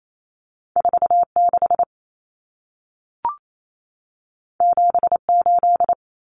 Morse code, heard as a keyed tone, about 28 wpm, 700 Hz. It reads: 46 78
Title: Numbers - 2-Digit 28wpm